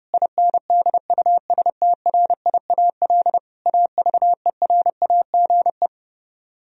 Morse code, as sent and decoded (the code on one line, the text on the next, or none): .. -. -.. ..- ... - .-. .. .- .-.. / .- ...- . .-. .- --. .
INDUSTRIAL AVERAGE